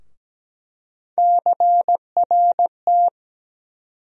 Morse code, heard as a keyed tone, about 17 wpm, 700 Hz. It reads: CRT
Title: USA Call Sign Postfixes - 3 Letters 17wpm